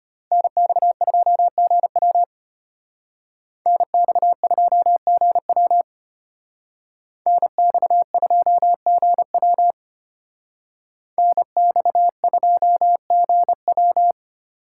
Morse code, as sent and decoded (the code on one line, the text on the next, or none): -. -..- ..--- --. .-- / -. -..- ..--- --. .-- / -. -..- ..--- --. .-- / -. -..- ..--- --. .--
NX2GW NX2GW NX2GW NX2GW